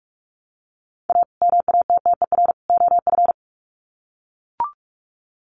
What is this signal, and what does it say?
Morse code, heard as a keyed tone, about 44 wpm, 700 Hz. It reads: A MATTER OF
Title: Sets of 3 Words 45wpm